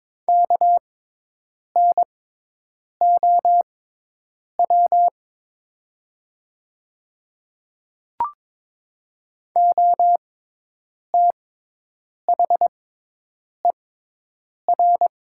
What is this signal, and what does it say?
Morse code, heard as a keyed tone, about 22 wpm, spaced as for 8 wpm, 700 Hz. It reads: KNOW OTHER
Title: 6x Character Spacing - Top 200 Words - 22wpm